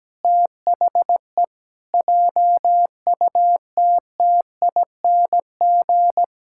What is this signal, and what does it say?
Morse code, heard as a keyed tone, about 17 wpm, 700 Hz. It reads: THE JUTTING
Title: Pangram Challenge v2 - 17wpm